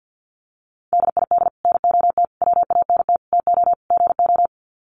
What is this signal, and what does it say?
Morse code, heard as a keyed tone, about 50 wpm, 700 Hz. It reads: DID NOT WANT TO GO
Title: Sets of 5 Words 50wpm